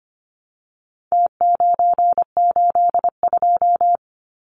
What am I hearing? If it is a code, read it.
Morse code, heard as a keyed tone, about 25 wpm, 700 Hz. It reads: T982
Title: Numbers - Contest Style - 4-Digit 25wpm